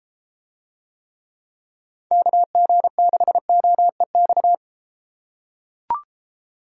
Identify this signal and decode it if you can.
Morse code, heard as a keyed tone, about 31 wpm, 700 Hz. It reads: KG6OEX